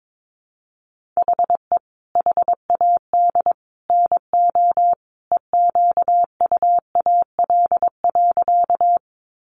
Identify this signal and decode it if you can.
Morse code, heard as a keyed tone, about 22 wpm, 700 Hz. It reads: HE HAD NO EQUAL.
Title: Sentences from Top 1400 Words 22wpm